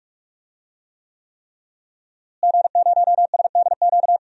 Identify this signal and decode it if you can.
Morse code, heard as a keyed tone, about 46 wpm, 700 Hz. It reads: G0SDQ
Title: DXCC-100 Call Signs - 5 Characters - Repeated 3x - 45wpm